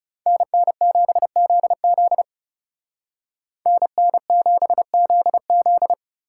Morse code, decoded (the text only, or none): NN7ZZ NN7ZZ